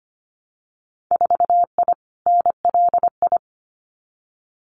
Morse code, heard as a keyed tone, about 25 wpm, 700 Hz. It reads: 4I NLI
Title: Winter Field Day Exchanges Encore 25wpm